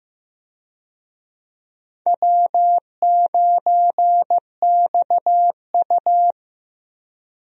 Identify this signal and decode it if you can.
Morse code, heard as a keyed tone, about 15 wpm, 700 Hz. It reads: W9XU
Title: USA Call Signs - 4 Characters - Repeated 2x - 15wpm